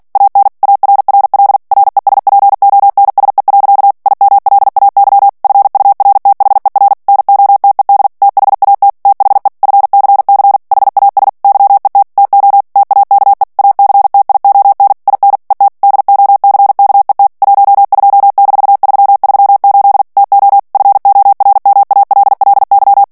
Morse code, heard as a keyed tone, about 48 wpm, 800 Hz. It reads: AN ARRL RESPONSE. IMLAY FURTHER NOTED THAT THE FCC HAS YET TO TAKE ACTION IN ET DOCKET 12-338 TO FORMALLY
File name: ANGLAIS texte clair 3 vitesse 45 mots minute